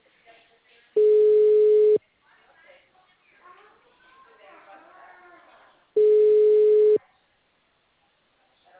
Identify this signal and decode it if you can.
Ringback tone (Europe, 425 Hz), with speech.